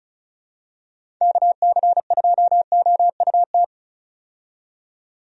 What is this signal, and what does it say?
Morse code, heard as a keyed tone, about 36 wpm, 700 Hz. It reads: KC2OUT
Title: USA Call Signs - 6 Characters - 35wpm